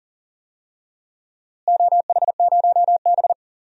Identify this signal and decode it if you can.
Morse code, heard as a keyed tone, about 39 wpm, 700 Hz. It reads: OH0B
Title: DXCC-100 Call Signs - 4 Characters - Repeated 3x - 40wpm